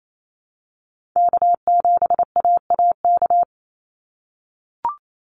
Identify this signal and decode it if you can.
Morse code, heard as a keyed tone, about 28 wpm, 700 Hz. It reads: K7AAK
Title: USA Call Signs 28wpm